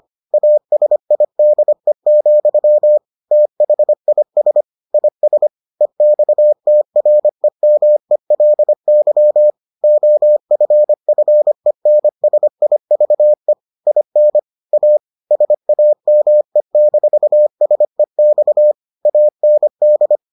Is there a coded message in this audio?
Yes: Morse code from 0.3 to 20.2 s, heard as a keyed tone, about 25 wpm, 600 Hz. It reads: ASIDE, THIS IS EXTREMELY OFFENSIVE IN A SAME-SEX AND